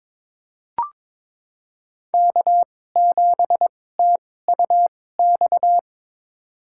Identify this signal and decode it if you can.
Morse code, heard as a keyed tone, about 22 wpm, spaced as for 16 wpm, 700 Hz. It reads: K7TUX
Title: USA Call Signs - 5 Characters - 2x Character Spacing - 22wpm